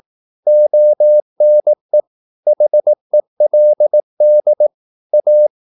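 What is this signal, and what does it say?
Morse code, heard as a keyed tone, about 18 wpm, 600 Hz. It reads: ONE HELD A
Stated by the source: Morse practice at 18 wpm, English text, from CW med Gnister.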